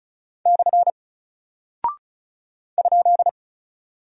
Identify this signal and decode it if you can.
Morse code, heard as a keyed tone, about 36 wpm, 700 Hz. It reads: / ?
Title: Single Character (Letter:Number:Punctuation:Prosign) Alt 35wpm